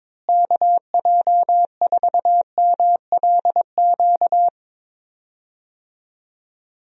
Morse code, heard as a keyed tone, about 22 wpm, 700 Hz. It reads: KJ4MLQ
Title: USA Call Signs - 6 Characters - Repeated 3x - 22wpm